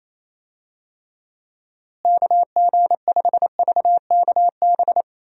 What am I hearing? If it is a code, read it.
Morse code, heard as a keyed tone, about 28 wpm, 700 Hz. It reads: KG5VKB
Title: USA Call Signs - 6 Characters - Repeated 3x - 28wpm